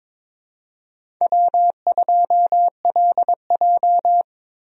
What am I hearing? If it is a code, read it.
Morse code, heard as a keyed tone, about 21 wpm, 700 Hz. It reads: W2LJ
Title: USA Call Signs - 4 Characters - 22wpm